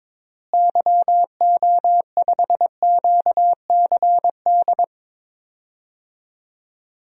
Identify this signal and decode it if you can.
Morse code, heard as a keyed tone, about 22 wpm, 700 Hz. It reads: YO5QCD
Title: DXCC-100 Call Signs - 6 Characters - Repeated 2x - 22wpm